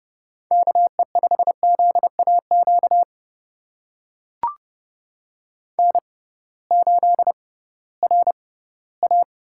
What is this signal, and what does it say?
Morse code, heard as a keyed tone, about 30 wpm, spaced as for 17 wpm, 700 Hz. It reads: KE5ZAQ N8RA
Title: USA Call Signs - 6x Character Spacing - 30wpm